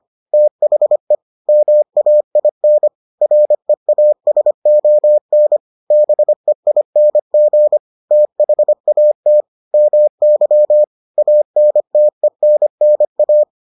Morse code, heard as a keyed tone, about 25 wpm, 600 Hz. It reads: THE MAIN REASON BEING THAT MY ANTENNA